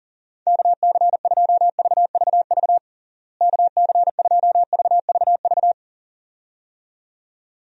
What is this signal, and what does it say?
Morse code, heard as a keyed tone, about 40 wpm, 700 Hz. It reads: KC2VVV KC2VVV
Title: USA Call Signs - 6 Characters - Repeated 2x - 40wpm